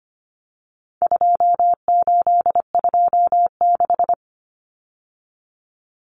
Morse code, heard as a keyed tone, about 25 wpm, 700 Hz. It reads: 2826
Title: Numbers - 4-Digit 25wpm